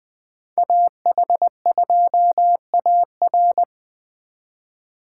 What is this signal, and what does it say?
Morse code, heard as a keyed tone, about 25 wpm, 700 Hz. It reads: AH2AR
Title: Speed Racing - USA 2x2 Call Signs 20wpm